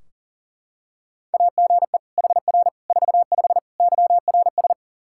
Morse code, heard as a keyed tone, about 40 wpm, 700 Hz. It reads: AGE HR 45 YRS
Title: Practice QSO - Exchange 3 - More Info - 40wpm